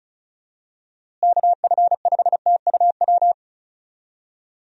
Morse code, heard as a keyed tone, about 34 wpm, 700 Hz. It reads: KF5TUW
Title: USA Call Signs - 6 Characters - Repeated 3x - 35wpm